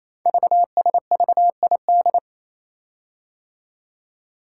Morse code, heard as a keyed tone, about 27 wpm, 700 Hz. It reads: VSVID
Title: Code Groups - 5 Letters - 28wpm